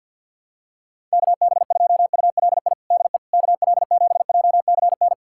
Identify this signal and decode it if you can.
Morse code, heard as a keyed tone, about 48 wpm, 700 Hz. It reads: KB2ULI DE KL7JCN
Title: USA Callsign DE Callsign 50wpm